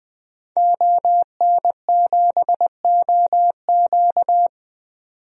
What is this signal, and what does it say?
Morse code, heard as a keyed tone, about 20 wpm, 700 Hz. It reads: ON7OQ
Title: DXCC-100 Call Signs - 5 Characters - 20wpm